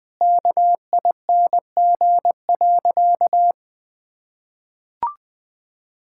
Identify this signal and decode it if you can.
Morse code, heard as a keyed tone, about 20 wpm, 700 Hz. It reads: KING.
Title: Sentences with 7-Word Phrases - 20wpm